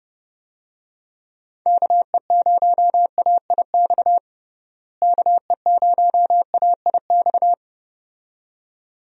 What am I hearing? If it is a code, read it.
Morse code, heard as a keyed tone, about 30 wpm, 700 Hz. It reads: KE0AIX KE0AIX
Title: USA Call Signs - 6 Characters - Repeated 2x - 30wpm